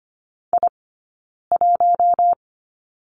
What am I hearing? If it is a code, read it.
Morse code, heard as a keyed tone, about 25 wpm, 700 Hz. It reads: I 1
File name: course / 25:10wpm / 092 Lesson 16d - Letter c - Two Characters 25wpm